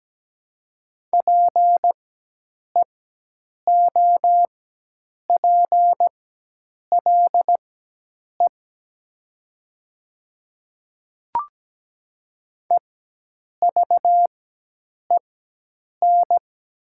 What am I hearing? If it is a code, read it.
Morse code, heard as a keyed tone, about 17 wpm, spaced as for 8 wpm, 700 Hz. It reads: PEOPLE EVEN
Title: 4x Character Spacing - Top 300 Words - 17wpm